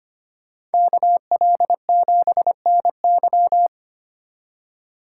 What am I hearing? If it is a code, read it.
Morse code, heard as a keyed tone, about 33 wpm, 700 Hz. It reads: KL7NY